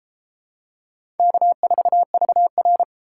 Morse code, heard as a keyed tone, about 34 wpm, 700 Hz. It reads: K4VR